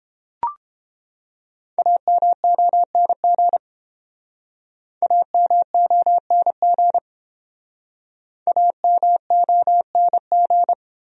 Morse code, heard as a keyed tone, about 30 wpm, 700 Hz. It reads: AMONG AMONG AMONG